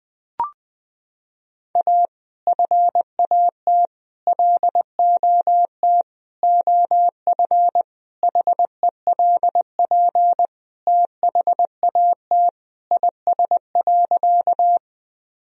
Morse code, heard as a keyed tone, about 20 wpm, 700 Hz. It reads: A FAT LOT OF HELP THAT IS.
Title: Sentences from Top 1000 Words - Review Part 4 - 20wpm